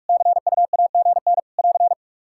Morse code, heard as a keyed tone, about 43 wpm, 700 Hz. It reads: CUAGN <AR>
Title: Practice QSO - Exchange 4 - Ending - 45wpm